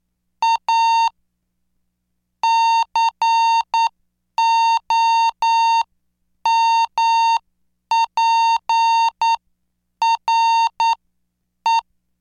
Morse code, heard as a keyed tone, about 9 wpm, spaced as for 8 wpm, 910 Hz. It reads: A COMPRE